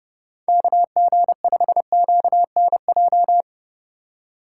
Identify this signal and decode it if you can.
Morse code, heard as a keyed tone, about 30 wpm, 700 Hz. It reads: KG5QNJ
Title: USA Call Signs - 6 Characters - 4x Character Spacing - 30wpm